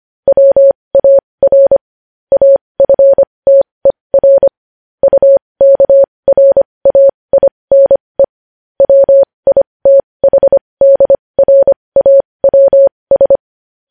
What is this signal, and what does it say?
Morse code, heard as a keyed tone, about 25 wpm, spaced as for 20 wpm, 550 Hz. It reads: WAR AFTER UKRAINE WITHDRAWS
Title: News Headlines at 20